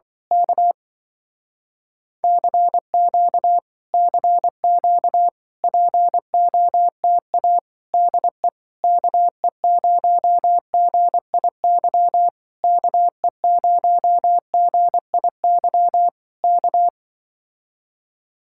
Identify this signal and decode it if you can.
Morse code, heard as a keyed tone, about 24 wpm, 700 Hz. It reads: K CQ CQ POTA DE KE0GIY KE0GIY K
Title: Speed-Racing Common QSO Phrases Encore - 20wpm